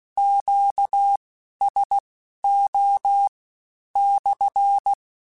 Morse code, heard as a keyed tone, about 16 wpm, spaced as for 11 wpm, 800 Hz. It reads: QSO /
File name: Q Calls / Q Calls-mp3s / Q Calls SPSL2 @15